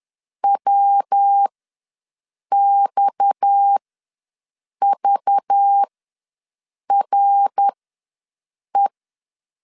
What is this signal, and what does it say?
Morse code, heard as a keyed tone, about 10 wpm, spaced as for 6 wpm, 800 Hz. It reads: WXVRE
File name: LEZIONI / 6 PKX0 / 6 PKX ws 10wpm 3706